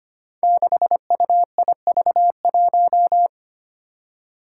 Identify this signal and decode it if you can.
Morse code, heard as a keyed tone, about 25 wpm, 700 Hz. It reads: 6UIV1
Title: Code Groups - 5 Letter:Numbers - 25wpm